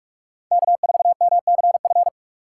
Morse code, heard as a keyed tone, about 46 wpm, 700 Hz. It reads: K4MCF
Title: USA Call Signs - 5 Characters - 4x Character Spacing - 45wpm